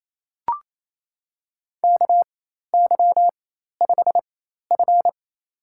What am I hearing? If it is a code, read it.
Morse code, heard as a keyed tone, about 28 wpm, spaced as for 13 wpm, 700 Hz. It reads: KY5F
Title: USA Call Signs - 4x Character Spacing - 28wpm